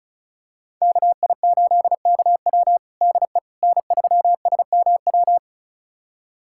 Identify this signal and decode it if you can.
Morse code, heard as a keyed tone, about 36 wpm, 700 Hz. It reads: KI8KW DE N3SMW
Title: USA Callsign DE Callsign - 5 Characters - 35wpm